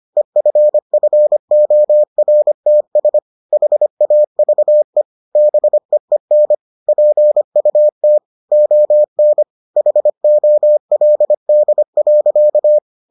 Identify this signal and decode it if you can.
Morse code, heard as a keyed tone, about 25 wpm, 600 Hz. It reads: EFFORTS HAVE BEEN PUT ON HOLD.